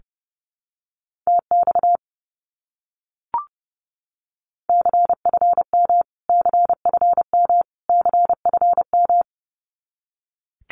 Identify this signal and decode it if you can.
Morse code, heard as a keyed tone, about 30 wpm, 700 Hz. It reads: TX CFM CFM CFM
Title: Abbreviations Repeated 3x - 30wpm